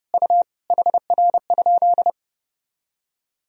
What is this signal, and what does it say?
Morse code, heard as a keyed tone, about 30 wpm, 700 Hz. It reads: U HR?
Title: Sentences from Top 300 Words and Abbreviations 30wpm